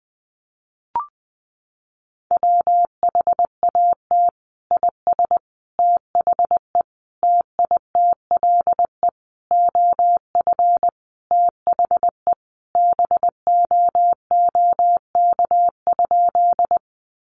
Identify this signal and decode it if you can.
Morse code, heard as a keyed tone, about 20 wpm, 700 Hz. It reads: WHAT IS THE TITLE OF THE BOOK?
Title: Sentences from Top 1200 Words 20wpm